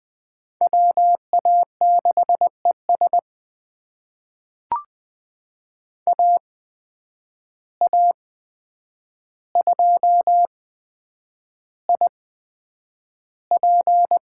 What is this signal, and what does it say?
Morse code, heard as a keyed tone, about 20 wpm, spaced as for 9 wpm, 700 Hz. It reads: WA6ES AA2IP
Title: USA Call Signs - 5 Characters - 8x Character Spacing - 20wpm